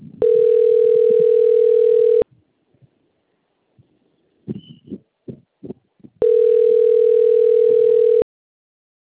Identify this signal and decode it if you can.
Ringback tone (North America), 440 + 480 Hz.